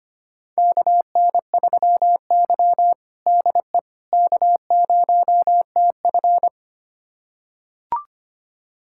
Morse code, heard as a keyed tone, about 25 wpm, 700 Hz. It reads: KN3Y DE K0TF
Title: USA Callsign DE Callsign - 4 Characters - 25wpm